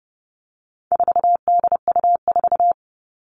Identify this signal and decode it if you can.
Morse code, heard as a keyed tone, about 30 wpm, 700 Hz. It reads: 4DU4
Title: All Cut Numbers - 4-Digit 30wpm